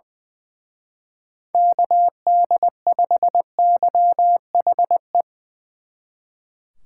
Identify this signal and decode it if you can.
Morse code, heard as a keyed tone, about 23 wpm, 700 Hz. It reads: KD5YHE
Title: Speed Racing - USA Call Signs 20wpm